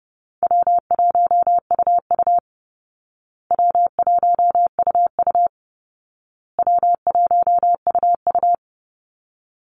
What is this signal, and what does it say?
Morse code, heard as a keyed tone, about 30 wpm, 700 Hz. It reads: W1UU W1UU W1UU
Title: Call Signs Repeated 2x - 30wpm